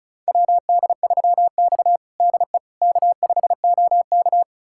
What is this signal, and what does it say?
Morse code, heard as a keyed tone, about 34 wpm, 690 Hz. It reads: WD3X DE K5OK
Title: USA Callsign DE Callsign - 4 Characters - 35wpm